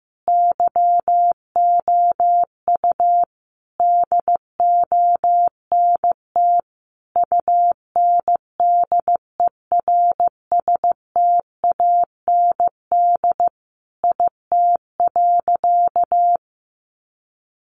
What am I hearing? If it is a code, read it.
Morse code, heard as a keyed tone, about 15 wpm, 700 Hz. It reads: YOU DONT UNDERSTAND IT.
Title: Sentences from Top 600 Words 15wpm